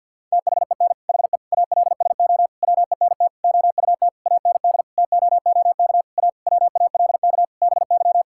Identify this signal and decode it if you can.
Morse code, heard as a keyed tone, about 51 wpm, 700 Hz. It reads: THEN HE ALSO WENT OUT AND TOOK A WALK BY
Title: Sentences from Top 1600 Words 50wpm